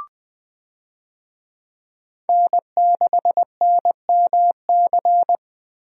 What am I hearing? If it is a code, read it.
Morse code, heard as a keyed tone, about 21 wpm, 700 Hz. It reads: N6NMC
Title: USA Call Signs - 5 Characters - Repeated 2x - 20wpm